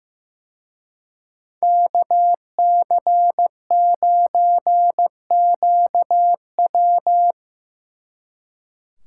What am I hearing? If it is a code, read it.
Morse code, heard as a keyed tone, about 17 wpm, 700 Hz. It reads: KC9QW
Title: Speed Racing - USA 2x2 Call Signs 15wpm